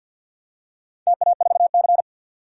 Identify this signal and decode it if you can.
Morse code, heard as a keyed tone, about 51 wpm, 700 Hz. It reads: TA4C
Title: DXCC-100 Call Signs - 4 Characters - 50wpm